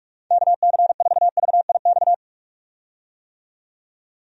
Morse code, heard as a keyed tone, about 46 wpm, 700 Hz. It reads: KC4VIX
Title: USA Call Signs - 6 Characters - 45wpm